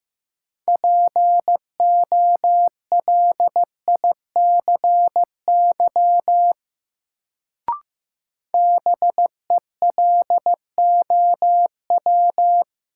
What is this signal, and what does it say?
Morse code, heard as a keyed tone, about 15 wpm, 700 Hz. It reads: POLICY BELOW
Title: Sentences from Top 1000 Words - Review Part 5 - New Word Drill - 15wpm